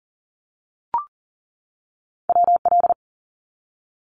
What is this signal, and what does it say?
Morse code, heard as a keyed tone, about 40 wpm, 700 Hz. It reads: WL
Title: Abbreviations 40wpm